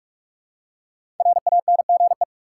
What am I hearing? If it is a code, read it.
Morse code, heard as a keyed tone, about 46 wpm, 700 Hz. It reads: RANGE
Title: Five Letter Words - 45wpm